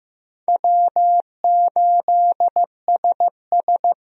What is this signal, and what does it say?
Morse code, heard as a keyed tone, about 15 wpm, 700 Hz. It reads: W8SS
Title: USA Call Signs - 4 Characters - 15wpm